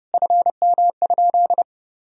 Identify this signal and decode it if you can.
Morse code, heard as a keyed tone, about 30 wpm, 700 Hz. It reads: FM?
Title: Sentences from Top 300 Words and Abbreviations 30wpm